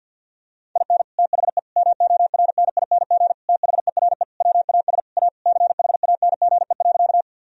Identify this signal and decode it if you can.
Morse code, heard as a keyed tone, about 50 wpm, 700 Hz. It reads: IN THE MORNING THERE WAS A CHANGE.
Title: Sentences from Top 1000 Words - Review Part 1 - 50wpm